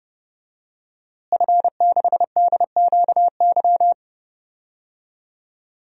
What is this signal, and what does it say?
Morse code, heard as a keyed tone, about 30 wpm, 700 Hz. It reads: F6DQY
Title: DXCC-100 Call Signs - 5 Characters - Repeated 3x - 30wpm